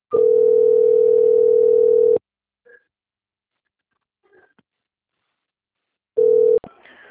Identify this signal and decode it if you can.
Ringback tone (North America), 440 + 480 Hz.